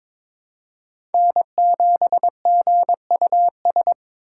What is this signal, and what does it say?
Morse code, heard as a keyed tone, about 22 wpm, 700 Hz. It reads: N7GUS
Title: USA Call Signs - 5 Characters - 8x Character Spacing - 22wpm